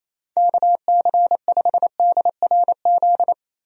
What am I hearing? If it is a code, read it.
Morse code, heard as a keyed tone, about 28 wpm, 700 Hz. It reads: KC5DRZ
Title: USA Call Signs - 6 Characters - Repeated 2x - 28wpm